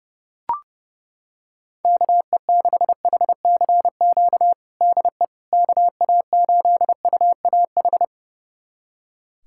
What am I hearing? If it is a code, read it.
Morse code, heard as a keyed tone, about 30 wpm, 700 Hz. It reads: KE6HCQ DE KA8UAH
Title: USA Callsign DE Callsign 30wpm